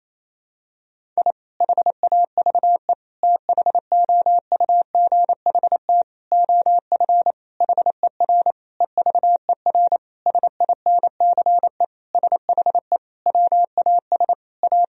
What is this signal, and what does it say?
Morse code, heard as a keyed tone, about 28 wpm, 700 Hz. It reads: I HAVE THOUGHT OF HER EVER SINCE SHE WAS A
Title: Sentences from Top 1000 Words - Review Part 1 - 28wpm